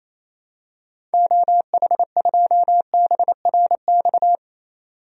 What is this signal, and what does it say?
Morse code, heard as a keyed tone, about 28 wpm, 700 Hz. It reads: OH2BRX
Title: DXCC-100 Call Signs - 6 Characters - 28wpm